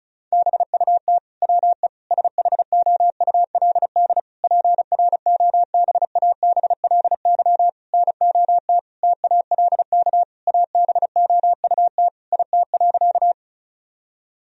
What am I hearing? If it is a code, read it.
Morse code, heard as a keyed tone, about 36 wpm, 700 Hz. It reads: BUT WE SHOULD PROBABLY NOT TALK ABOUT IT.
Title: Sentences from Top 1000 Words - Review Part 4 - 35wpm